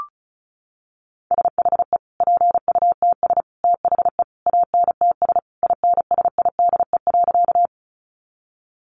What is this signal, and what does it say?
Morse code, heard as a keyed tone, about 34 wpm, 700 Hz. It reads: SHE PUTS THE ANTS INSIDE.
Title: Easy Phrases 35wpm